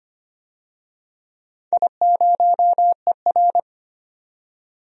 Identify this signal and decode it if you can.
Morse code, heard as a keyed tone, about 25 wpm, 700 Hz. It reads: I0ER
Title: DXCC-100 Call Signs - 4 Characters - 25wpm